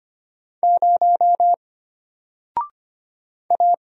Morse code, heard as a keyed tone, about 25 wpm, 700 Hz. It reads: 0 A
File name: Single-Letter-Number-25wpm